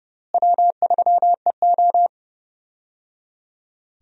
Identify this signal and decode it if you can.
Morse code, heard as a keyed tone, about 31 wpm, 700 Hz. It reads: W3EO
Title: USA Call Signs - 4 Characters - Repeated 3x - 30wpm